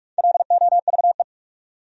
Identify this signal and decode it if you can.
Morse code, heard as a keyed tone, about 54 wpm, 690 Hz. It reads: LOVE
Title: Speed-Racing Top 500 Words 45wpm